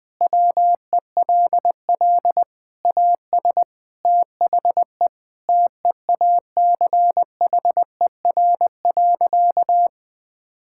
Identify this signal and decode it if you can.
Morse code, heard as a keyed tone, about 20 wpm, 700 Hz. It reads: WELL AS THE TEACHER.
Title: Sentences with 4-Word Phrases - 20wpm